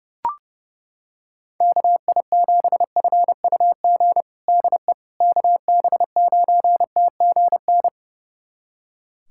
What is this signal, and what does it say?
Morse code, heard as a keyed tone, about 30 wpm, 700 Hz. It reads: KI7FUG DE KB9TGN
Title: USA Callsign DE Callsign - 6 Characters - 30wpm